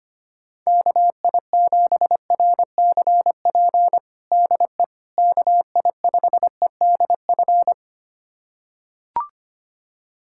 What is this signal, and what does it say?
Morse code, heard as a keyed tone, about 25 wpm, 700 Hz. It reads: KI7RCP DE KI5EDF
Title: USA Callsign DE Callsign 25wpm